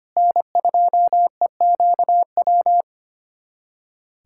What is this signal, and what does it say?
Morse code, heard as a keyed tone, about 25 wpm, 700 Hz. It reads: N2EQW
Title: USA Callsign DE Callsign - 5 Characters - 25wpm